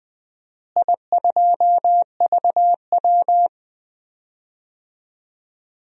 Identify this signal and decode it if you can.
Morse code, heard as a keyed tone, about 20 wpm, 700 Hz. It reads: I2VW
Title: DXCC-100 Call Signs - 4 Characters - Repeated 2x - 20wpm